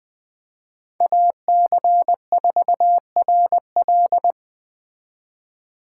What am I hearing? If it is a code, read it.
Morse code, heard as a keyed tone, about 20 wpm, 700 Hz. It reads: AC4RL